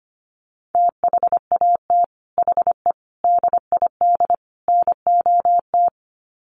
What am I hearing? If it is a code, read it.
Morse code, heard as a keyed tone, about 25 wpm, 700 Hz. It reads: THAT HE DID NOT
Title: Sets of 4 Words 25wpm